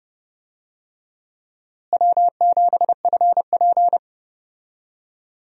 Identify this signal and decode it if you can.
Morse code, heard as a keyed tone, about 30 wpm, 700 Hz. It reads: W7FP
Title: USA Call Signs - 4 Characters - Repeated 3x - 30wpm